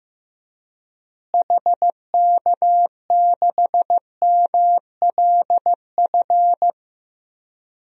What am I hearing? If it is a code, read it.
Morse code, heard as a keyed tone, about 16 wpm, 700 Hz. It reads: HK6MLF
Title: DXCC-100 Call Signs - 6 Characters - Repeated 2x - 15wpm